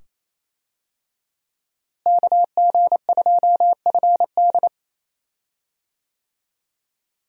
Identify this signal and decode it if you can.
Morse code, heard as a keyed tone, about 28 wpm, 700 Hz. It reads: KG2FD
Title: USA Call Signs - 5 Characters - Repeated 2x - 28wpm